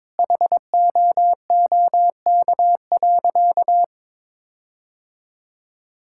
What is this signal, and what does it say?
Morse code, heard as a keyed tone, about 22 wpm, 700 Hz. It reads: HOOK.
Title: Sentences from Top 1600 Words 22wpm